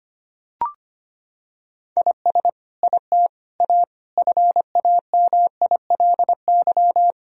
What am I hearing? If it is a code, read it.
Morse code, heard as a keyed tone, about 25 wpm, 700 Hz. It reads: IS IT A FAMILY
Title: Sentences from Top 1000 Words - Review Part 4 - 25wpm